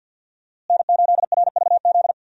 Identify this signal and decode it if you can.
Morse code, heard as a keyed tone, about 52 wpm, 700 Hz. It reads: N8RVZ
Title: USA Call Signs - 5 Characters - Repeated 2x - 50wpm